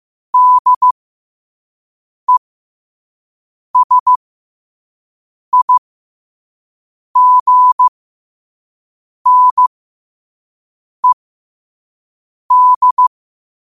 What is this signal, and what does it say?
Morse code, heard as a keyed tone, about 15 wpm, spaced as for 5 wpm, 1000 Hz. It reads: DESIGNED
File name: qotdfiles / 5wpm